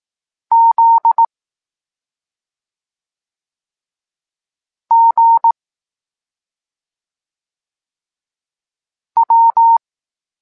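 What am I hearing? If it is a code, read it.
Morse code, heard as a keyed tone, about 18 wpm, 920 Hz. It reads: Z G W
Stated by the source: Evenly spaced Morse lesson text 21 for learning letters G and Z, from SP9MOA Niepołomicki Klub Krótkofalowców.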